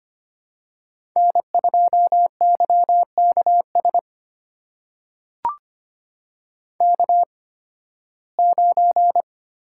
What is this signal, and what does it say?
Morse code, heard as a keyed tone, about 25 wpm, spaced as for 16 wpm, 700 Hz. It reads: N2YKS K9
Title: USA Call Signs - 5 Characters - 8x Character Spacing - 25wpm